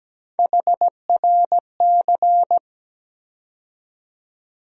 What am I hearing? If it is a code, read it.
Morse code, heard as a keyed tone, about 17 wpm, 700 Hz. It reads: HRC
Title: USA Call Sign Postfixes - 3 Letters 17wpm